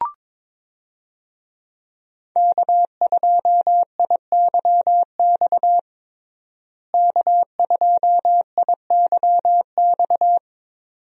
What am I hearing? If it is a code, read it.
Morse code, heard as a keyed tone, about 22 wpm, 700 Hz. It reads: K2IYX K2IYX
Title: USA Call Signs - 5 Characters - Repeated 2x - 22wpm